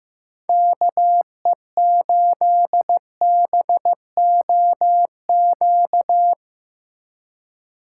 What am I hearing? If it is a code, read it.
Morse code, heard as a keyed tone, about 16 wpm, 700 Hz. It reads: KE8BOQ
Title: Speed Racing - USA 2x3 Call Signs 15wpm